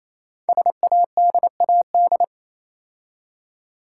Morse code, heard as a keyed tone, about 28 wpm, 700 Hz. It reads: SADAD
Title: Code Groups - 5 Letters - 28wpm